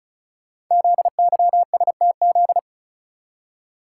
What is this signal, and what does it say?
Morse code, heard as a keyed tone, about 36 wpm, 700 Hz. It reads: ZYSTZ
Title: Code Groups - 5 Letters - 35wpm